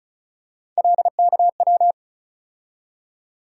Morse code, heard as a keyed tone, about 36 wpm, 700 Hz. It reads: LKW
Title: USA Call Sign Postfixes - 3 Letters 35wpm